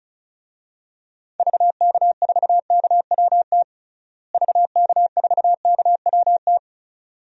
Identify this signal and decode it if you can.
Morse code, heard as a keyed tone, about 36 wpm, 700 Hz. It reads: VK4KWT VK4KWT
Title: DXCC-100 Call Signs - 6 Characters - Repeated 2x - 35wpm